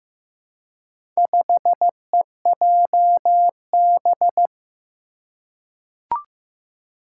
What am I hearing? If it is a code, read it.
Morse code, heard as a keyed tone, about 15 wpm, 700 Hz. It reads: 5EJB
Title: Four Challenging Letter-Number Combinations 15wpm